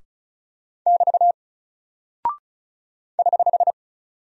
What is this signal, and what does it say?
Morse code, heard as a keyed tone, about 34 wpm, 700 Hz. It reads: <BT> <HH>
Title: Single Character (Letter:Number:Punctuation:Prosign) 35wpm